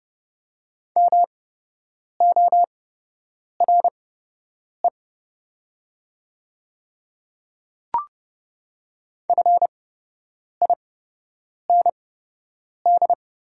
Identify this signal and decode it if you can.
Morse code, heard as a keyed tone, about 29 wpm, spaced as for 8 wpm, 700 Hz. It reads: MORE FIND
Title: 8x Character Spacing - Top 100 Words - 30wpm